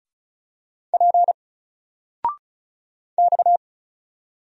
Morse code, heard as a keyed tone, about 36 wpm, 700 Hz. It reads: P X
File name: Single-Letter-Number-35wpm